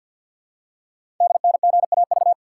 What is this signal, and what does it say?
Morse code, heard as a keyed tone, about 48 wpm, 690 Hz. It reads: DNGAV
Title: Code Groups - 5 Letters - 50wpm